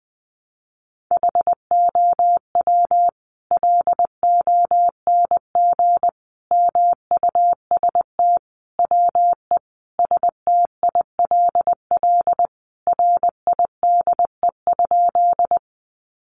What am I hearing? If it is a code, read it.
Morse code, heard as a keyed tone, about 20 wpm, 700 Hz. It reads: HOW LONG MUST WE STILL RIDE?
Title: Sentences from Top 1200 Words 20wpm